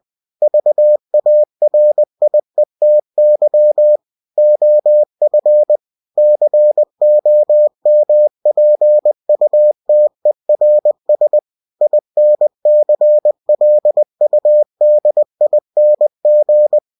Morse code, heard as a keyed tone, about 20 wpm, 600 Hz. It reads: VARIETY OF COMPUTERS INCLUDING
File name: cw-books / winnie-the-pooh / winnie-20wpm-page-0120